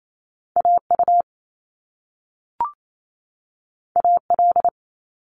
Morse code, heard as a keyed tone, about 28 wpm, 700 Hz. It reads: AU AL
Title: Top Two Letter Combinations in Words 28wpm